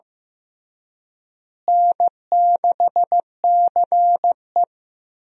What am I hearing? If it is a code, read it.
Morse code, heard as a keyed tone, about 15 wpm, 700 Hz. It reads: N6CE
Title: USA Call Signs - 4 Characters - Repeated 3x - 15wpm